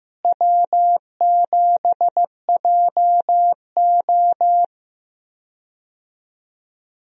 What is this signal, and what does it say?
Morse code, heard as a keyed tone, about 15 wpm, 700 Hz. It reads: W7JO
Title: USA Call Signs - 4 Characters - Repeated 2x - 15wpm